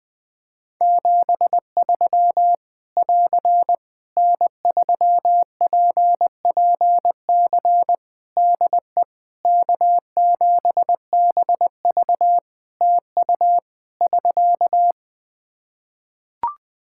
Morse code, heard as a keyed tone, about 20 wpm, 700 Hz. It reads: 73 <AR> N3PPC DE K7BV TU <SK>
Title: Practice QSO - Exchange 4 - Ending - 20wpm